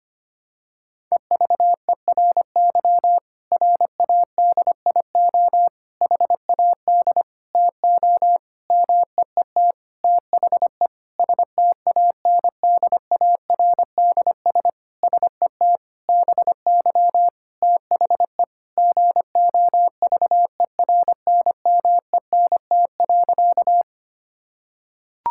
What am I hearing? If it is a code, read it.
Morse code, heard as a keyed tone, about 25 wpm, 700 Hz. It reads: EVERY RADIO HAD TO MEET THE STANDARDS SET BY THE GOVERNMENT.
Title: Sentences from Top 1000 Words - Review Part 6 - 25wpm